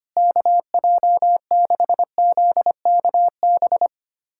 Morse code, heard as a keyed tone, about 25 wpm, 700 Hz. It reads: KJ6ZKB
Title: USA Callsign DE Callsign 25wpm